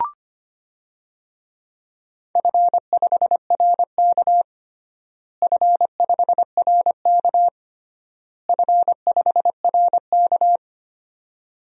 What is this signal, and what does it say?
Morse code, heard as a keyed tone, about 25 wpm, 700 Hz. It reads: F5RK F5RK F5RK
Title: DXCC-100 Call Signs - 4 Characters - Repeated 3x - 25wpm